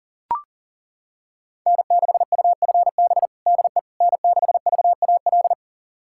Morse code, heard as a keyed tone, about 40 wpm, 700 Hz. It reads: N6UFB DE N6VAL
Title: USA Callsign DE Callsign - 5 Characters - 40wpm